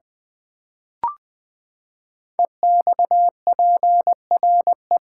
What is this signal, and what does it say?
Morse code, heard as a keyed tone, about 20 wpm, 700 Hz. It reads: EXPRE
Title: Build A Word - 5 Letters At A Time - 10-Letter Words - 20wpm